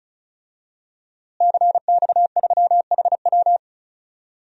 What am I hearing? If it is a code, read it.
Morse code, heard as a keyed tone, about 34 wpm, 700 Hz. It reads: CX3HW
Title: DXCC-100 Call Signs - 5 Characters - Repeated 3x - 35wpm